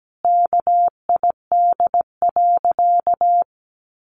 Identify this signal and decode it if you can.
Morse code, heard as a keyed tone, about 17 wpm, 700 Hz. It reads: KID.
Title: A Sound Of Thunder 17wpm